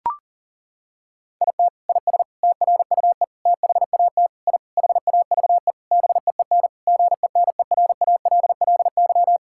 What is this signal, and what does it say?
Morse code, heard as a keyed tone, about 40 wpm, 700 Hz. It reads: IT IS TRUE THAT I HAVE BEEN GENERALLY
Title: Sentences from Top 1000 Words - Review Part 6 - 40wpm